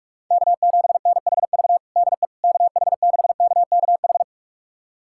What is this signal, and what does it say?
Morse code, heard as a keyed tone, about 42 wpm, 700 Hz. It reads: K7NHV DE KH6KKH
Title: USA Callsign DE Callsign 45wpm